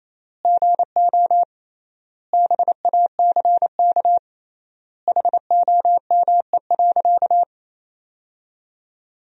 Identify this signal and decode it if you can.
Morse code, heard as a keyed tone, about 27 wpm, 700 Hz. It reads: GO BACK HOME.
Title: 3x Word-Spacing - Sentences from Top 500 Words - 28wpm